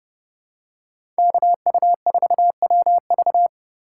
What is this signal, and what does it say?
Morse code, heard as a keyed tone, about 30 wpm, 700 Hz. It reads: KU4WV
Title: USA Call Signs - 5 Characters - 3x Character Spacing - 30wpm